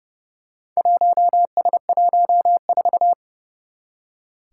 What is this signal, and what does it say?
Morse code, heard as a keyed tone, about 30 wpm, 700 Hz. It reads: 1S14
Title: Four Challenging Letter-Number Combinations 30wpm